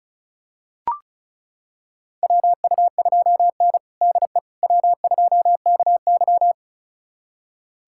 Morse code, heard as a keyed tone, about 36 wpm, 700 Hz. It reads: WU2N DE W2KY
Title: USA Callsign DE Callsign - 4 Characters - 35wpm